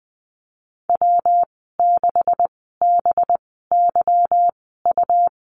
Morse code, heard as a keyed tone, about 20 wpm, spaced as for 15 wpm, 700 Hz. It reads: W6BYU
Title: 193 Lesson 31g - Letter v - Call Signs 20wpm